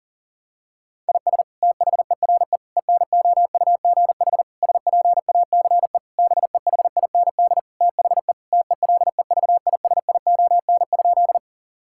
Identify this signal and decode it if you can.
Morse code, heard as a keyed tone, about 40 wpm, 700 Hz. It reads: IS THERE ENOUGH SPACE BEHIND THE TELEVISION?
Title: Sentences from Top 1000 Words - Review Part 5 - 40wpm